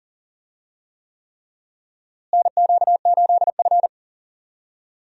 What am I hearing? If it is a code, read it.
Morse code, heard as a keyed tone, about 40 wpm, 700 Hz. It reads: NQ8F
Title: USA Call Signs - 4 Characters - Repeated 3x - 40wpm